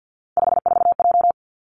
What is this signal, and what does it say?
Morse code, heard as a keyed tone, about 50 wpm, 700 Hz. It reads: 54J
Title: Three Challenging Letter-Number Combinations 50wpm